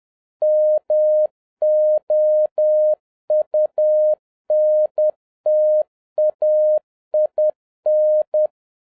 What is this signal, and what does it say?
Morse code, heard as a keyed tone, about 10 wpm, 610 Hz. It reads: MOUNTAIN